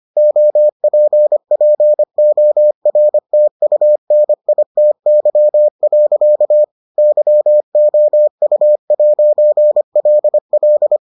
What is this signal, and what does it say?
Morse code, heard as a keyed tone, about 25 wpm, 600 Hz. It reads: OPPORTUNITY. YOU'LL